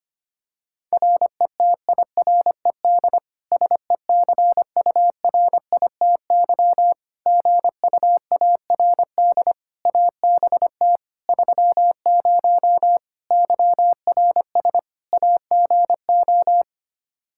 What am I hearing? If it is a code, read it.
Morse code, heard as a keyed tone, about 25 wpm, 700 Hz. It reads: RETIRED SECURITY GUARD ABT 30 YRS AGO
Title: K1USN SST QSO Practice 25wpm